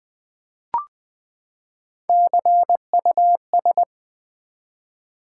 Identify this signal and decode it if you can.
Morse code, heard as a keyed tone, about 20 wpm, 700 Hz. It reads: CUS
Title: Build A Word - 3 Letters At A Time - 8-Letter Words - 20wpm